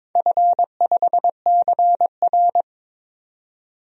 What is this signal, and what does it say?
Morse code, heard as a keyed tone, about 22 wpm, 700 Hz. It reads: F5CR
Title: DXCC-100 Call Signs - 4 Characters - 22wpm